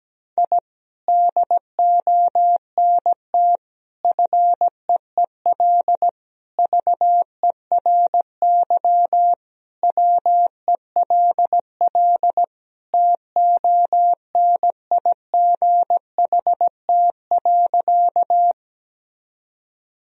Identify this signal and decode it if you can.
Morse code, heard as a keyed tone, about 17 wpm, 700 Hz. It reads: I DONT FEEL VERY WELL TONIGHT.
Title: Sentences from Top 1000 Words - Review Part 3 - 17wpm